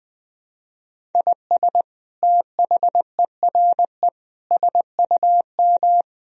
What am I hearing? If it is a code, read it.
Morse code, heard as a keyed tone, about 20 wpm, 700 Hz. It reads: IS THERE SUM
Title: Sentences from Top 200 Words and Abbreviations 20wpm